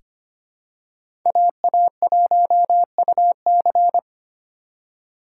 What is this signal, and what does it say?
Morse code, heard as a keyed tone, about 25 wpm, 700 Hz. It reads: AA1UC
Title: K1USN SST QSO Practice - 25wpm